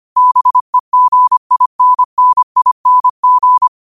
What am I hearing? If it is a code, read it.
Morse code, heard as a keyed tone, about 25 wpm, 1000 Hz. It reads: BEGINNING